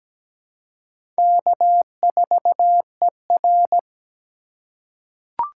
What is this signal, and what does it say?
Morse code, heard as a keyed tone, about 17 wpm, 700 Hz. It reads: K4ER
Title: USA Call Signs - 4 Characters - 4x Character Spacing - 17wpm